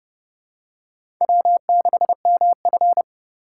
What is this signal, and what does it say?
Morse code, heard as a keyed tone, about 32 wpm, 690 Hz. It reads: W6MF
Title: USA Call Signs - 4 Characters - 6x Character Spacing - 30wpm